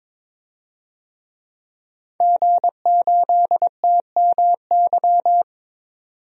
Morse code, heard as a keyed tone, about 21 wpm, 700 Hz. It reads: G8TMY
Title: DXCC-100 Call Signs - 5 Characters - 22wpm